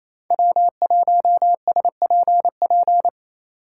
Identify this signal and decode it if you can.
Morse code, heard as a keyed tone, about 28 wpm, 700 Hz. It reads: W1SPP
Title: USA Call Signs - 5 Characters - 4x Character Spacing - 28wpm